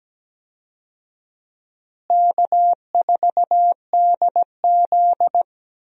Morse code, heard as a keyed tone, about 17 wpm, 700 Hz. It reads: K4DZ
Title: USA Call Signs - 4 Characters - Repeated 2x - 17wpm